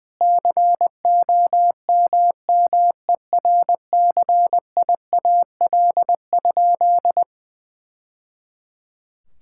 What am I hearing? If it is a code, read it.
Morse code, heard as a keyed tone, about 20 wpm, 700 Hz. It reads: COMMERCIAL?
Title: Sentences from Top 1000 Words - Review Part 5 - 20wpm